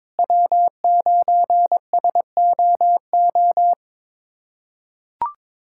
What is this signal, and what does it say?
Morse code, heard as a keyed tone, about 22 wpm, 700 Hz. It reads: W9SOO
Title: USA Callsign DE Callsign - 5 Characters - 22wpm